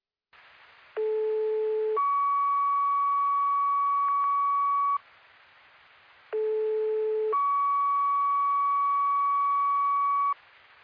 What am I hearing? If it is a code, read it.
Two-tone page, heard as a one-second tone, then a three-second tone, about 430 then 1130 Hz, about 430 then 1130 Hz. It is heard from 1.0 to 10.3 s.